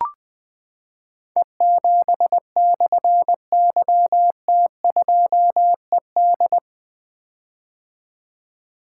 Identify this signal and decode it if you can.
Morse code, heard as a keyed tone, about 20 wpm, 700 Hz. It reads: E7/YT2ED
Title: Difficult Call Signs 20wpm